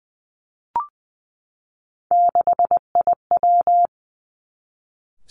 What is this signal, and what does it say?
Morse code, heard as a keyed tone, about 20 wpm, 700 Hz. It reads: 6IW
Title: Three Challenging Letter-Number Combinations 20wpm